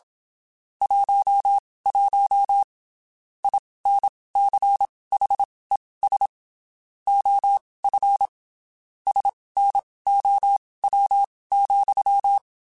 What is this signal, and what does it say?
Morse code, heard as a keyed tone, about 27 wpm, spaced as for 19 wpm, 800 Hz. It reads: 11 INCHES OF SNOW,
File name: (spaced) / QSOs Spaced-mp3s / WA7QOX 1 @25 sp